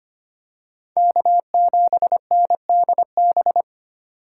Morse code, heard as a keyed tone, about 25 wpm, 700 Hz. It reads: K7NDB
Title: USA Call Signs - 3x Character Spacing - 25wpm